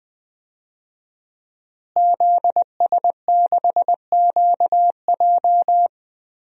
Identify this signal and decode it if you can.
Morse code, heard as a keyed tone, about 20 wpm, 700 Hz. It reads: ZS6QJ
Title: DXCC-100 Call Signs - 5 Characters - Repeated 2x - 20wpm